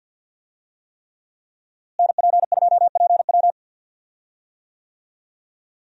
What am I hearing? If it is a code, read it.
Morse code, heard as a keyed tone, about 52 wpm, 700 Hz. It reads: NP2PW
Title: USA Call Signs - 5 Characters - Repeated 3x - 50wpm